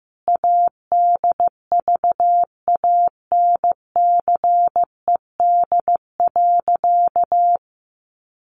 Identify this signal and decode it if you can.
Morse code, heard as a keyed tone, about 15 wpm, 700 Hz. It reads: ADVANCED.
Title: Sentences from Top 1400 Words 15wpm